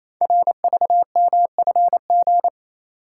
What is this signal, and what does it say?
Morse code, heard as a keyed tone, about 28 wpm, 700 Hz. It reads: RVMFG
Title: Code Groups - 5 Letter:Numbers - 28wpm